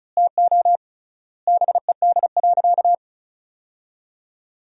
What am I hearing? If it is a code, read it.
Morse code, heard as a keyed tone, about 36 wpm, 700 Hz. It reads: TO BED.
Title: 3x Word-Spacing - Sentences from Top 500 Words - 35wpm